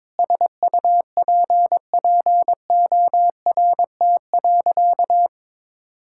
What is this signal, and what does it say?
Morse code, heard as a keyed tone, about 22 wpm, 700 Hz. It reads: SUPPORT.
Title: Sentences from Top 1000 Words - Review Part 5 - 22wpm